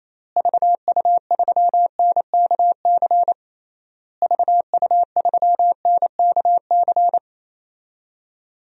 Morse code, heard as a keyed tone, about 28 wpm, 700 Hz. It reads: VU3NKC VU3NKC
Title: DXCC-100 Call Signs - 6 Characters - Repeated 3x - 28wpm